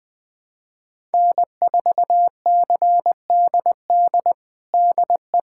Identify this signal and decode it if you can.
Morse code, heard as a keyed tone, about 20 wpm, 700 Hz. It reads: N4CDD DE
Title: USA Callsign DE Callsign - 5 Characters - 20wpm